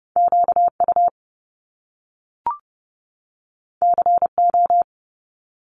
Morse code, heard as a keyed tone, about 30 wpm, 700 Hz. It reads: QU CO
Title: Top Two Letter Combinations in Words 30wpm